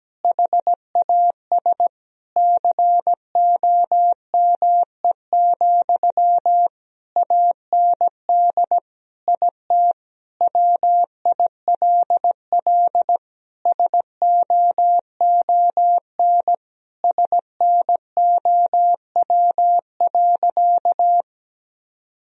Morse code, heard as a keyed tone, about 17 wpm, 700 Hz. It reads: HAS COME, AND IT WILL SOON SNOW.
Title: Sentences from Top 1000 Words - Review Part 3 - 17wpm